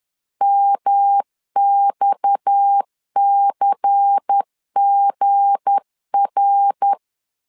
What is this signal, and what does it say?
Morse code, heard as a keyed tone, about 10 wpm, 800 Hz. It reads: MXCGR
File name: LEZIONI / 8 YJ45 / 8 YJ4 10wpm 2633